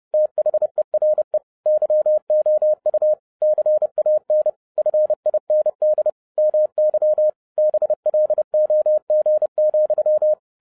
Morse code, heard as a keyed tone, about 30 wpm, 610 Hz. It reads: THERE YOU CAN FIND MY BLOG,